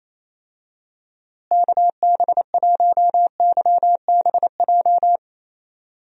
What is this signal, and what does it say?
Morse code, heard as a keyed tone, about 27 wpm, 700 Hz. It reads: KB1YBJ
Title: USA Call Signs - 6 Characters - Repeated 2x - 28wpm